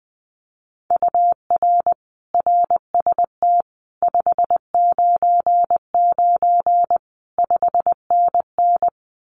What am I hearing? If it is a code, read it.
Morse code, heard as a keyed tone, about 20 wpm, 700 Hz. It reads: UR RST 599 5NN
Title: Common QSO Phrases Encore 20wpm